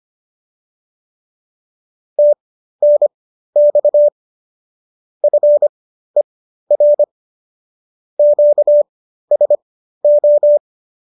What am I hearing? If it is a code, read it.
Morse code, heard as a keyed tone, about 25 wpm, spaced as for 13 wpm, 590 Hz. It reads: TNX FER QSO